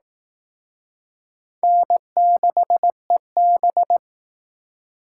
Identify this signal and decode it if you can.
Morse code, heard as a keyed tone, about 18 wpm, 700 Hz. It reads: N6EB
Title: Speed Racing - USA 1x2 Call Signs 15wpm